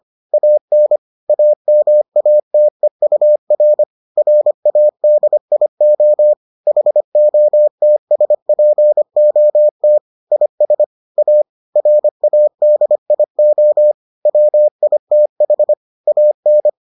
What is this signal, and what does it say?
Morse code, heard as a keyed tone, about 25 wpm, 600 Hz. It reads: AN AMATEUR RADIO HOTSPOT IS A RADIO WITH AN